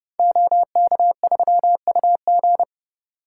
Morse code, heard as a keyed tone, about 28 wpm, 700 Hz. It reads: OK3UG
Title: Code Groups - 5 Letter:Numbers - 30wpm